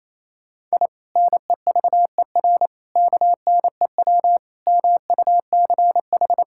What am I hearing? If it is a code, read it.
Morse code, heard as a keyed tone, about 28 wpm, 700 Hz. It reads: I NEVER KNEW MUCH
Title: Sentences from Top 1000 Words - Review Part 6 - 28wpm